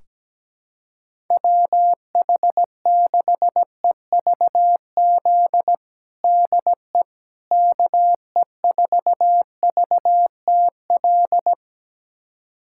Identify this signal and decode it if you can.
Morse code, heard as a keyed tone, about 17 wpm, 700 Hz. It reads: WH6EVZ DE KE4VTL
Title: USA Callsign DE Callsign - 6 Characters - 17wpm